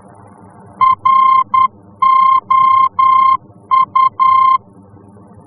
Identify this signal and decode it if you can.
Morse code, heard as a keyed tone, about 10 wpm, 1050 Hz. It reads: ROU
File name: france / mx / ROU-F-VOR